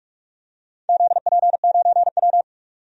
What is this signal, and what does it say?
Morse code, heard as a keyed tone, about 47 wpm, 700 Hz. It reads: ZP9W
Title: DXCC-100 Call Signs - 4 Characters - 45wpm